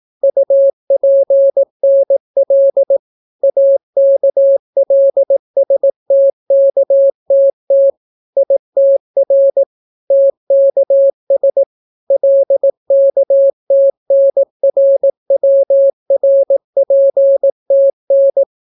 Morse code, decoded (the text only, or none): UPNL AKLSTKTT ITR TKS LKTNRWRPTN